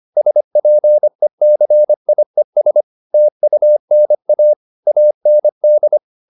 Morse code, heard as a keyed tone, about 25 wpm, 600 Hz. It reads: SPECIES TUNA AND
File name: the old man and the sea part 1 25wpm